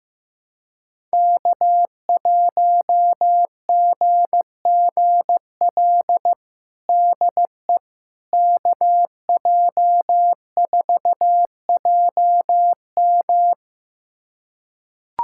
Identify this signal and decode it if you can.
Morse code, heard as a keyed tone, about 16 wpm, 700 Hz. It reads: K1GGL DE KJ4JM
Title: USA Callsign DE Callsign - 5 Characters - 15wpm